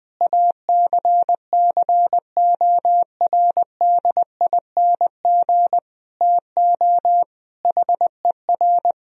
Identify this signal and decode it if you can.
Morse code, heard as a keyed tone, about 20 wpm, 700 Hz. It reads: ACCORDING TO HER
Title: Sentences from Top 1000 Words - Review Part 4 - 20wpm